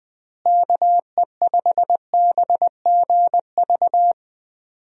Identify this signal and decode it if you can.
Morse code, heard as a keyed tone, about 20 wpm, 700 Hz. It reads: KE5BGV
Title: USA Call Signs - 6 Characters - Repeated 3x - 20wpm